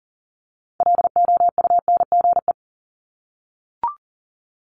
Morse code, heard as a keyed tone, about 38 wpm, 700 Hz. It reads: LOUNGE
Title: Multi-letter Phonemes Repeated 3x 40wpm